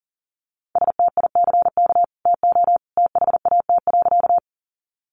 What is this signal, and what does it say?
Morse code, heard as a keyed tone, about 40 wpm, 700 Hz. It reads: STICK TO THAT.
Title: Sentences from Top 1200 Words 40wpm